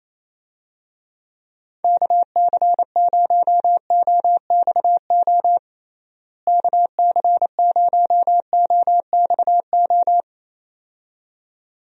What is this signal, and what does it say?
Morse code, heard as a keyed tone, about 28 wpm, 700 Hz. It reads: KC0OXO KC0OXO
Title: USA Call Signs - 6 Characters - Repeated 2x - 28wpm